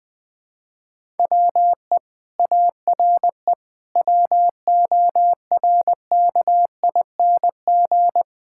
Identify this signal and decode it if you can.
Morse code, heard as a keyed tone, about 21 wpm, 700 Hz. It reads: WE ARE WORKING
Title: Sentences from Top 1000 Words - Review Part 6 - 20wpm